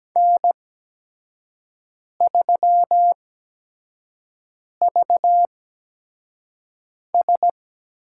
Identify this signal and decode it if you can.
Morse code, heard as a keyed tone, about 17 wpm, spaced as for 5 wpm, 700 Hz. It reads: N3VS
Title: USA Call Signs - 8x Character Spacing - 17wpm